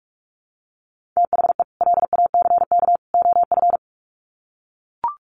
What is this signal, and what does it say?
Morse code, heard as a keyed tone, about 44 wpm, 700 Hz. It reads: THE LACK OF
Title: Sets of 3 Words 45wpm